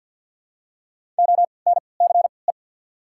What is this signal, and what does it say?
Morse code, heard as a keyed tone, about 48 wpm, spaced as for 28 wpm, 690 Hz. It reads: ONCE
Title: 3x Character Spacing - Top 500 Words - 50wpm